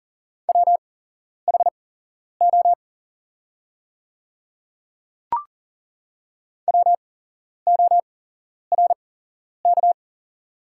Morse code, heard as a keyed tone, about 40 wpm, spaced as for 11 wpm, 700 Hz. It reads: WHO WORK
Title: 8x Character Spacing - Top 300 Words - 40wpm